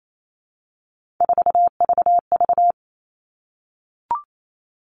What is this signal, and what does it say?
Morse code, heard as a keyed tone, about 28 wpm, 700 Hz. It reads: 4VV
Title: All Cut Numbers - 3-Digit 28wpm